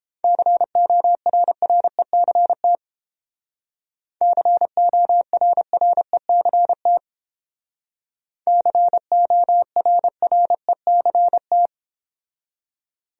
Morse code, heard as a keyed tone, about 30 wpm, 700 Hz. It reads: CORRECT CORRECT CORRECT